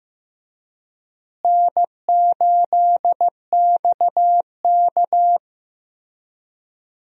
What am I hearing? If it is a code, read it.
Morse code, heard as a keyed tone, about 15 wpm, 700 Hz. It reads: N8XK
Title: USA Call Signs - 4 Characters - Repeated 3x - 15wpm